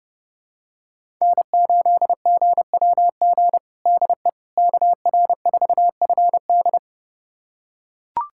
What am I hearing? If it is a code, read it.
Morse code, heard as a keyed tone, about 32 wpm, 700 Hz. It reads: N8GWG DE KR4FD
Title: USA Callsign DE Callsign - 5 Characters - 30wpm